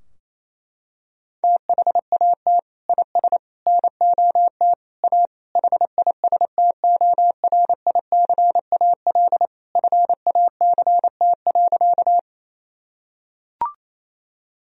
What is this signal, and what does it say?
Morse code, heard as a keyed tone, about 28 wpm, 700 Hz. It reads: THAT IS NOT A HISTORICAL FACT.
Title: Sentences from Top 1000 Words - Review Part 4 - 28wpm